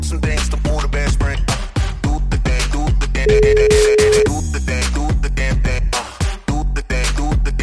Ringback tone (Europe, 425 Hz), with noise.